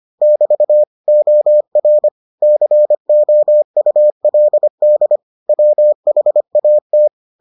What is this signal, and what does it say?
Morse code, heard as a keyed tone, about 25 wpm, 600 Hz. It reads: <BT> OR COULD WHAT